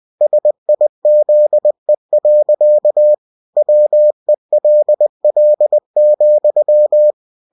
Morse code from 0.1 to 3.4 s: SIZE.